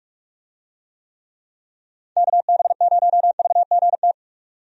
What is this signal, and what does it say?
Morse code, heard as a keyed tone, about 46 wpm, 700 Hz. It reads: KB0VGT
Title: USA Call Signs - 6 Characters - 45wpm